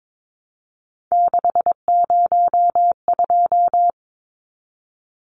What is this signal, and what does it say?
Morse code, heard as a keyed tone, about 22 wpm, 700 Hz. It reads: 602
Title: Numbers - 3-Digit 22wpm